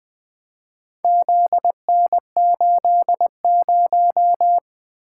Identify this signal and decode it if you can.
Morse code, heard as a keyed tone, about 20 wpm, 700 Hz. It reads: ZN80
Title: Four Letter-Number Combinations - 20wpm